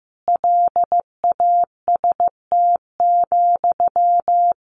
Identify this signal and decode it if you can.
Morse code, heard as a keyed tone, about 15 wpm, 700 Hz. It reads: LAST,
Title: Sentences from Top 300 Words 15wpm